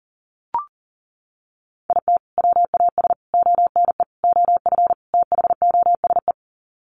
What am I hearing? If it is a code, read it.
Morse code, heard as a keyed tone, about 40 wpm, 700 Hz. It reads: IT WAS ONE OF THOSE
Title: Sets of 5 Words Encore 40wpm